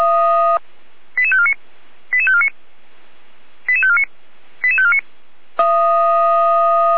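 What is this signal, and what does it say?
Five-tone selective call (ZVEI), heard as quick runs of five tones, tones of about 70 ms. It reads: 88549 88549 88549 88549
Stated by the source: ZVEI five-tone call 88483 received with an SDR.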